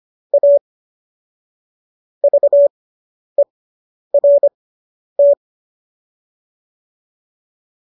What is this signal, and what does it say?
Morse code, heard as a keyed tone, about 25 wpm, spaced as for 10 wpm, 570 Hz. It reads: A VERT